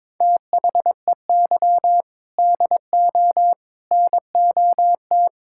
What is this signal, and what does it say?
Morse code, heard as a keyed tone, about 22 wpm, 700 Hz. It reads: THEY DO NOT
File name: practice / 22wpm / Build A Phrase - 5-Word Phrases - 22wpm